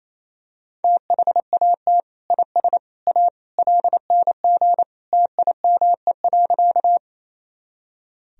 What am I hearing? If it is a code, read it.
Morse code, heard as a keyed tone, about 27 wpm, 700 Hz. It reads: THAT IS A LNG TIME.
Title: Sentences from Top 100 Words and Abbreviations 28wpm